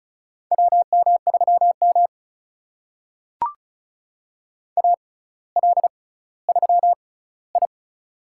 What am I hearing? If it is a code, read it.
Morse code, heard as a keyed tone, about 36 wpm, spaced as for 18 wpm, 700 Hz. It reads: WM3M AL3I
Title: USA Call Signs - 4 Characters - 6x Character Spacing - 35wpm